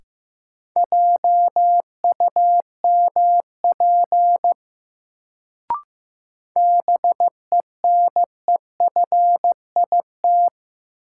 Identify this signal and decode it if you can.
Morse code, heard as a keyed tone, about 15 wpm, 700 Hz. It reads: JUMP BENEFIT
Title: Sentences from Top 1300 Words - New Word Drill Alt - 15wpm